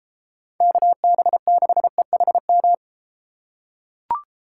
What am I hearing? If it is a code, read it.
Morse code, heard as a keyed tone, about 33 wpm, 700 Hz. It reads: KB6EHM